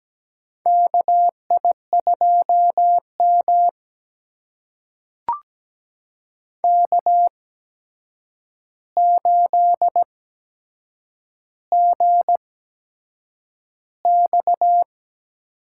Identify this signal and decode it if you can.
Morse code, heard as a keyed tone, about 17 wpm, spaced as for 7 wpm, 700 Hz. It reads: KI2M K8GX
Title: USA Call Signs - 4 Characters - 8x Character Spacing - 17wpm